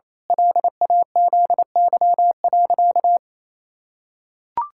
Morse code, heard as a keyed tone, about 28 wpm, 700 Hz. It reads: LAZY.
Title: Pangram Challenge v2 - 28wpm